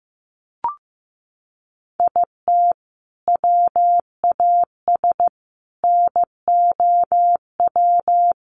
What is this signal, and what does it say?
Morse code, heard as a keyed tone, about 15 wpm, 700 Hz. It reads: IT WAS NOW
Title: Sentences from Top 200 Words 15wpm